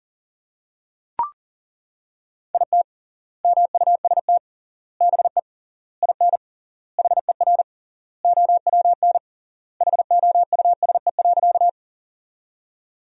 Morse code, heard as a keyed tone, about 40 wpm, 700 Hz. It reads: IT MUST BE IN HER OWN HOUSE.
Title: 3x Word-Spacing - Sentences from Top 300 Words - 40wpm